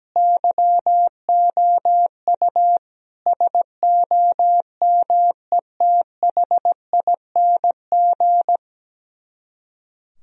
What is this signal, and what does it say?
Morse code, heard as a keyed tone, about 17 wpm, 700 Hz. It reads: YOU SOMETHING
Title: Speed-Racing - Sets of 5 Words 17wpm